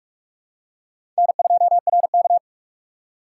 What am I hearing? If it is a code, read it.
Morse code, heard as a keyed tone, about 45 wpm, 690 Hz. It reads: N2RK
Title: USA Call Signs - 4 Characters - 3x Character Spacing - 45wpm